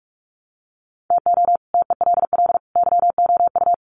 Morse code, heard as a keyed tone, about 44 wpm, 700 Hz. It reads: TO TELL YOU
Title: Sets of 3 Words 45wpm